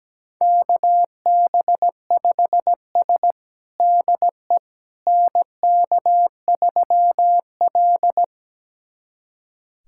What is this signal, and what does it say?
Morse code, heard as a keyed tone, about 17 wpm, 700 Hz. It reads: KB5S DE NK3L
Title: USA Callsign DE Callsign - 4 Characters - 17wpm